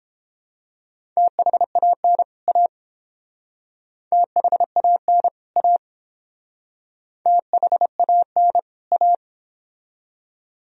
Morse code, heard as a keyed tone, about 29 wpm, 700 Hz. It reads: THAN A THAN A THAN A